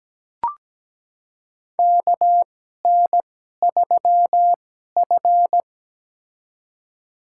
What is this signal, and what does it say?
Morse code, heard as a keyed tone, about 17 wpm, spaced as for 12 wpm, 700 Hz. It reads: KN3F
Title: USA Call Signs - 2x Character Spacing - 17wpm